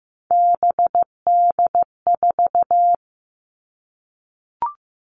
Morse code, heard as a keyed tone, about 15 wpm, 700 Hz. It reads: BD4
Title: Callsign Prefixes for Top 100 DXCC - 15wpm